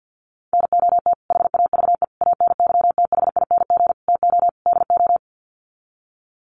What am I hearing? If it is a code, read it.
Morse code, heard as a keyed tone, about 50 wpm, 700 Hz. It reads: NOT HAVE ANYTHING TO DO
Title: Sets of 5 Words Encore 50wpm